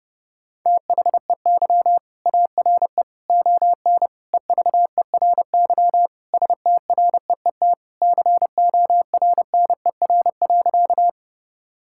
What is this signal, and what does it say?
Morse code, heard as a keyed tone, about 30 wpm, 700 Hz. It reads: THEY ARE ON EVERY STREET CORNER.
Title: Sentences from Top 1600 Words 30wpm